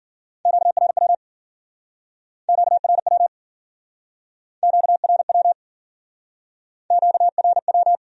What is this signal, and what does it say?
Morse code, heard as a keyed tone, about 50 wpm, 700 Hz. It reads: QRW QRW QRW QRW
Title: Speed-Racing Q-Codes - 40wpm